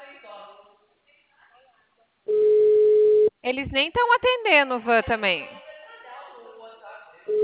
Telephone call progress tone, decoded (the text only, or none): ringback tone